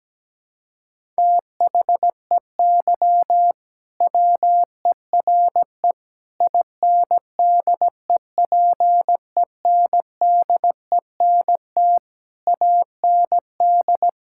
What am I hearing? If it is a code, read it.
Morse code, heard as a keyed tone, about 17 wpm, 700 Hz. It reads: THEY WERE INDEPENDENT AND
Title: Sentences from Top 1000 Words - Review Part 6 - 17wpm